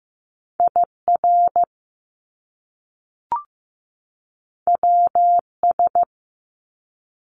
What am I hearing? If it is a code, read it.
Morse code, heard as a keyed tone, about 15 wpm, 700 Hz. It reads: IR WS
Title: Top Two Letter Combinations in Words 15wpm